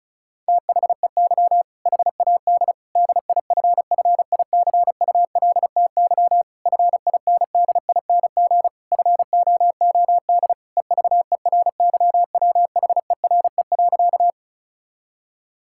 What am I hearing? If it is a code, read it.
Morse code, heard as a keyed tone, about 36 wpm, 700 Hz. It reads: THEY HAD DIFFICULTY FINDING FOOD EVERYWHERE.
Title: Sentences from Top 1600 Words 35wpm